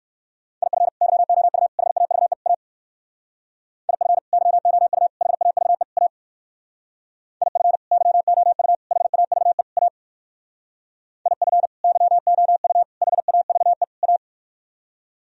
Morse code, heard as a keyed tone, about 58 wpm, 700 Hz. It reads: IF YOU HAVE A IF YOU HAVE A IF YOU HAVE A IF YOU HAVE A